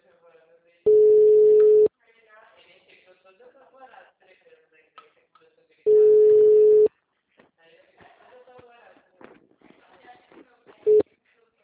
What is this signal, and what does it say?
Ringback tone (Europe, 425 Hz), with speech.
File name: out- -2041- - - .23